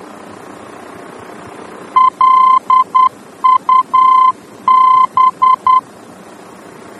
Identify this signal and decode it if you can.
Morse code, heard as a keyed tone, about 10 wpm, 1030 Hz. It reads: LUB
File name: germany / mx / LUB-D-VOR